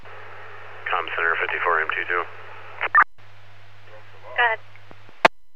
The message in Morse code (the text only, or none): none